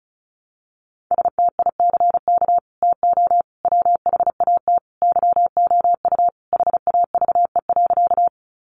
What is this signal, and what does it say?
Morse code, heard as a keyed tone, about 34 wpm, 700 Hz. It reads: STICK TO WHAT YOU HAVE.
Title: Sentences from Top 1200 Words 35wpm